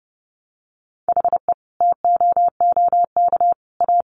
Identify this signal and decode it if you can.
Morse code, heard as a keyed tone, about 30 wpm, 700 Hz. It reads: HE TOOK A
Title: Sentences from Top 1000 Words 30wpm